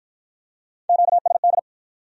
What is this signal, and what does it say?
Morse code, heard as a keyed tone, about 53 wpm, 700 Hz. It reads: QSD